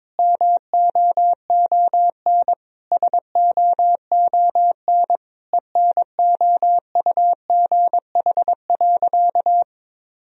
Morse code, heard as a keyed tone, about 22 wpm, 700 Hz. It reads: MOON SOON ENOUGH.
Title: Sentences from Top 1000 Words - Review Part 3 - 22wpm